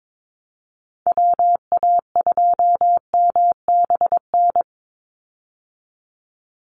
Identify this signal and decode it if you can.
Morse code, heard as a keyed tone, about 22 wpm, 700 Hz. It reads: WA2MBN
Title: USA Call Signs 22wpm